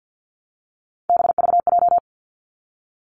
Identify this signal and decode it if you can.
Morse code, heard as a keyed tone, about 50 wpm, 700 Hz. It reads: BVJ
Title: Three Challenging Letter-Number Combinations 50wpm